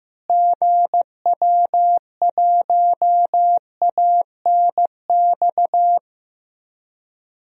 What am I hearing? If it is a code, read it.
Morse code, heard as a keyed tone, about 15 wpm, 700 Hz. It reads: GW1ANX
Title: DXCC-100 Call Signs - 6 Characters - Repeated 3x - 15wpm